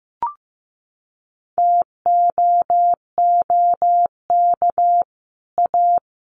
Morse code, heard as a keyed tone, about 14 wpm, 700 Hz. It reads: TOOK A
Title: Sets of 5 Words Encore 15wpm